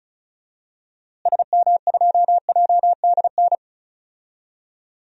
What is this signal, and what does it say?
Morse code, heard as a keyed tone, about 36 wpm, 700 Hz. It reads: SM2JDN
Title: DXCC-100 Call Signs - 6 Characters - Repeated 2x - 35wpm